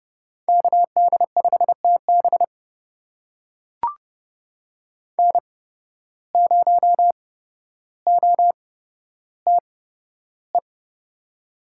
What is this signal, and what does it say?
Morse code, heard as a keyed tone, about 30 wpm, spaced as for 13 wpm, 700 Hz. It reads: KD5TB N0OTE
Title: USA Call Signs - 5 Characters - 8x Character Spacing - 30wpm